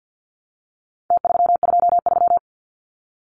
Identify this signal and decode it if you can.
Morse code, heard as a keyed tone, about 50 wpm, 700 Hz. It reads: T323
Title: Numbers - Contest Style - 4-Digit 50wpm